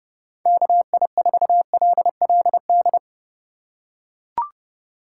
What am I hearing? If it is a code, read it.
Morse code, heard as a keyed tone, about 31 wpm, 700 Hz. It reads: KI4LLD
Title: USA Call Signs - 4x Character Spacing - 30wpm